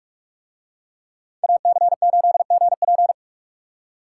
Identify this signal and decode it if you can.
Morse code, heard as a keyed tone, about 44 wpm, 700 Hz. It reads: AC8GP